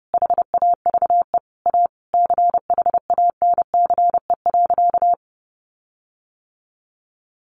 Morse code, heard as a keyed tone, about 30 wpm, 700 Hz. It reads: HAVE A CHANCE.
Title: Sentences from Top 1400 Words 30wpm